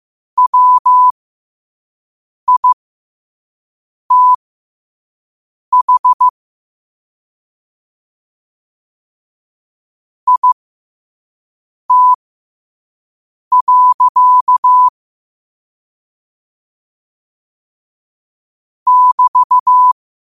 Morse code, heard as a keyed tone, about 15 wpm, spaced as for 5 wpm, 1000 Hz. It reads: WITH IT. <BT>
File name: qotdfiles / 5wpm